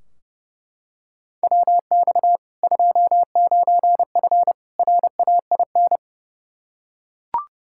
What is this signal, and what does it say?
Morse code, heard as a keyed tone, about 30 wpm, 700 Hz. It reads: WX 29F RAIN
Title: K1USN SST QSO Practice 30wpm